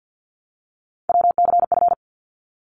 Morse code, heard as a keyed tone, about 50 wpm, 700 Hz. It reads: WCF
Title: ARRL Sections 50wpm